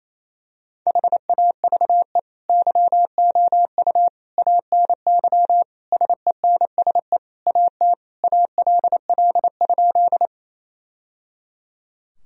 Morse code, heard as a keyed tone, about 28 wpm, 700 Hz. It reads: HAVE YOU ANY SENSE AT ALL?
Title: Sentences from Top 1000 Words - Review Part 4 - 28wpm